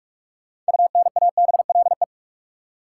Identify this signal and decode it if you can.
Morse code, heard as a keyed tone, about 41 wpm, 700 Hz. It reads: UNABLE
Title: Six Letter Words - 45wpm